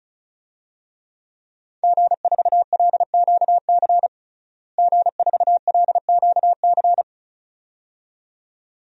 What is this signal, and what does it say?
Morse code, heard as a keyed tone, about 34 wpm, 700 Hz. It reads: G4LQC G4LQC
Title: DXCC-100 Call Signs - 5 Characters - Repeated 2x - 35wpm